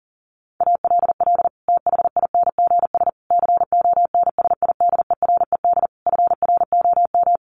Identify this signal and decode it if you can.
Morse code, heard as a keyed tone, about 37 wpm, 700 Hz. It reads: ALL THINGS CONSIDERED FROM
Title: Sets of 5 Words Encore 40wpm